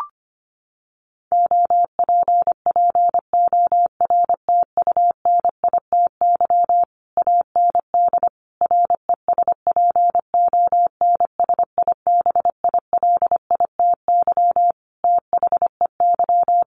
Morse code, heard as a keyed tone, about 25 wpm, 700 Hz. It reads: OPPORTUNITY AND RESPONSIBILITY THEY
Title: Sentences from Top 1200 Words 25wpm